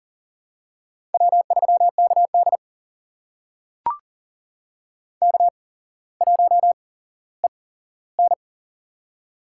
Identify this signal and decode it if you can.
Morse code, heard as a keyed tone, about 37 wpm, spaced as for 17 wpm, 700 Hz. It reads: W3KD K1EN
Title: USA Call Signs - 4 Characters - 8x Character Spacing - 40wpm